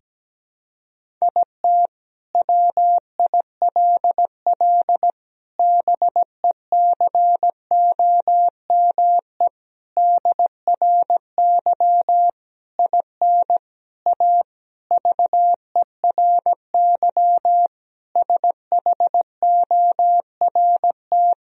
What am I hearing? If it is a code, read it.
Morse code, heard as a keyed tone, about 17 wpm, 700 Hz. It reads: IT WILL BECOME DRY IN A VERY SHORT
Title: Sentences from Top 1000 Words - Review Part 3 - 17wpm